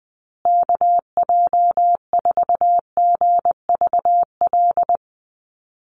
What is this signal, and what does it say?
Morse code, heard as a keyed tone, about 20 wpm, 700 Hz. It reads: KJ4GVL
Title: USA Call Signs Repeated 3x 20wpm